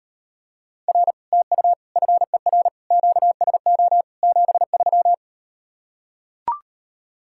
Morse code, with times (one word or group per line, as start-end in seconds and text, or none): 0.9-1.1 R
1.3-1.7 TU
2.0-2.7 FER
2.9-4.0 QSO
4.2-5.1 73